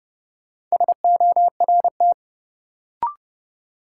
Morse code, heard as a keyed tone, about 30 wpm, 700 Hz. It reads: SORT
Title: Sentences from Top 1300 Words - New Word Drill Alt - 30wpm